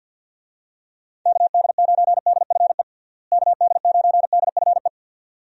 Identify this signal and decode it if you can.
Morse code, heard as a keyed tone, about 52 wpm, 700 Hz. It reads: KD9DFE KD9DFE
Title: USA Call Signs - 6 Characters - Repeated 2x - 50wpm